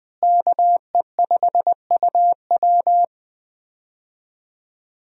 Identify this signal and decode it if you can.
Morse code, heard as a keyed tone, about 20 wpm, 700 Hz. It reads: KE5UW
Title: USA Call Signs - 5 Characters - Repeated 2x - 20wpm